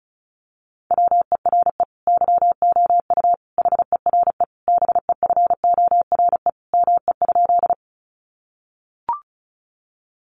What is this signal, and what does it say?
Morse code, heard as a keyed tone, about 34 wpm, 700 Hz. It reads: WERE YOU HERE BEFORE ME?
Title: Sentences from Top 200 Words 35wpm